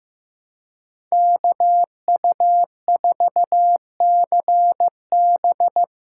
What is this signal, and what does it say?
Morse code, heard as a keyed tone, about 15 wpm, 700 Hz. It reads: KU4CB
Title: USA Call Signs - 5 Characters - 3x Character Spacing - 15wpm